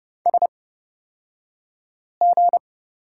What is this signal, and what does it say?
Morse code, heard as a keyed tone, about 30 wpm, spaced as for 5 wpm, 700 Hz. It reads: SG
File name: Lesson 36d - BT - Two Characters 30wpm